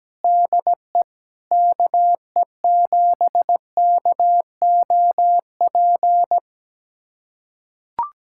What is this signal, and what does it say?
Morse code, heard as a keyed tone, about 17 wpm, 700 Hz. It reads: DE KE7KOP
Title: USA Callsign DE Callsign - 6 Characters - 17wpm